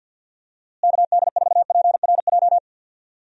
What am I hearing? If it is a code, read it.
Morse code, heard as a keyed tone, about 47 wpm, 700 Hz. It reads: KD4PRJ
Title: USA Call Signs - 6 Characters - Repeated 2x - 50wpm